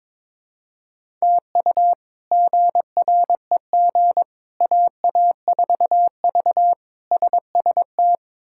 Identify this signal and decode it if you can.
Morse code, heard as a keyed tone, about 22 wpm, 700 Hz. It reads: TU GREG AA4V SST
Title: K1USN SST QSO Practice - 22wpm